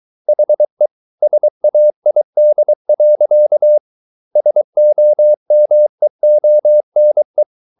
Morse code, heard as a keyed tone, about 23 wpm, 600 Hz. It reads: HE SAID. SOMEONE